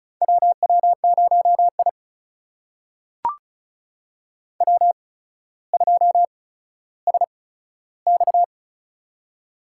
Morse code, heard as a keyed tone, about 36 wpm, spaced as for 15 wpm, 700 Hz. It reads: WW0I W2SX
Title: USA Call Signs - 8x Character Spacing - 35wpm